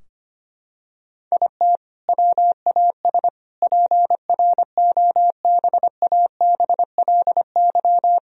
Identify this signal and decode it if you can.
Morse code, heard as a keyed tone, about 25 wpm, 700 Hz. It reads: IT WAS PROBABLY
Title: Sentences from Top 1000 Words - Review Part 4 - 25wpm